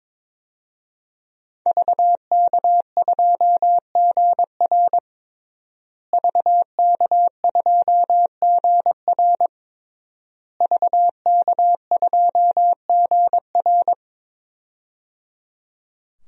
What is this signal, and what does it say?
Morse code, heard as a keyed tone, about 22 wpm, 700 Hz. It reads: VK2GR VK2GR VK2GR
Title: DXCC-100 Call Signs - 5 Characters - Repeated 3x - 22wpm